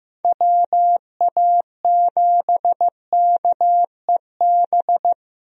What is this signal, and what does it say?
Morse code, heard as a keyed tone, about 15 wpm, 700 Hz. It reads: WA7KEB
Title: USA Call Signs - 6 Characters - Repeated 2x - 15wpm